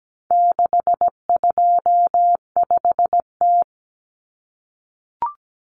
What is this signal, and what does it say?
Morse code, heard as a keyed tone, about 17 wpm, 700 Hz. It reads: 625T
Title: Common Cut Numbers - 4-Digit 17wpm